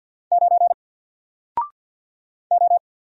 Morse code, heard as a keyed tone, about 50 wpm, 700 Hz. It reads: 9 O
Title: Single Letter-Number 50wpm